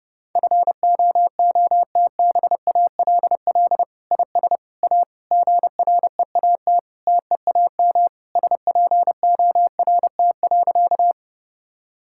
Morse code, heard as a keyed tone, about 32 wpm, 700 Hz. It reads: FOOTBALL IS A GREAT TEAM SPORT.
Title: Sentences from Top 1600 Words 30wpm